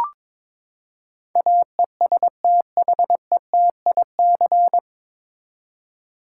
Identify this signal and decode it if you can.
Morse code, heard as a keyed tone, about 22 wpm, 700 Hz. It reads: AESTHETIC
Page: Morse Code Ninja practice 22wpm